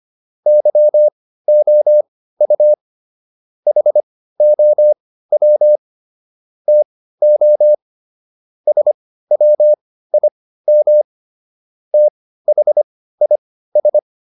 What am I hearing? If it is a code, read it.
Morse code, heard as a keyed tone, about 25 wpm, spaced as for 15 wpm, 600 Hz. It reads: YOU HOW TO SWIM THIS